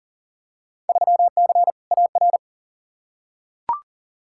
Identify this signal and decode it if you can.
Morse code, heard as a keyed tone, about 40 wpm, 700 Hz. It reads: 3C AR
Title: Field Day Exchanges Encore - 40wpm